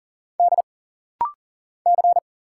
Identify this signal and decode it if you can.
Morse code, heard as a keyed tone, about 40 wpm, 700 Hz. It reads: D C
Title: Single Letters - Rapid-Fire 40wpm